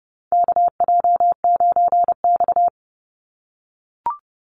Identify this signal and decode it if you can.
Morse code, heard as a keyed tone, about 31 wpm, 700 Hz. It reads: KJ9X
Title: NA SOTA Chaser Callsigns 30wpm